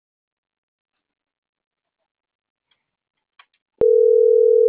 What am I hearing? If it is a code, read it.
Ringback tone (North America), 440 + 480 Hz.
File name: CENTRAL MAERSK / out martin.blas